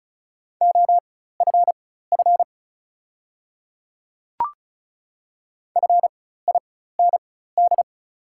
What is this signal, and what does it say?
Morse code, heard as a keyed tone, about 36 wpm, spaced as for 16 wpm, 690 Hz. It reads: OFF FIND
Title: 4x Character Spacing - Top 200 Words - 35wpm